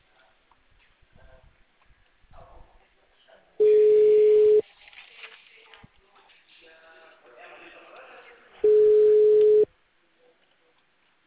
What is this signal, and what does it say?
Ringback tone (Europe, 425 Hz), with speech.
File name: out- -2002- - - .76281